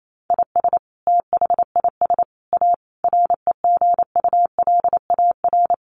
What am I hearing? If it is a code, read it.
Morse code, heard as a keyed tone, about 28 wpm, 700 Hz. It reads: IS THIS A REGULAR
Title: Sentences from Top 900 Words 28wpm